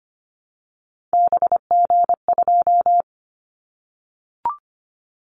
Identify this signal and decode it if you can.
Morse code, heard as a keyed tone, about 25 wpm, 700 Hz. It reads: BG2
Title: Callsign Prefixes for Top 100 DXCC - 25wpm